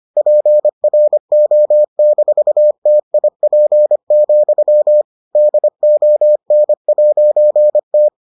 Morse code, heard as a keyed tone, about 25 wpm, 600 Hz. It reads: PRO-TIP, DON'T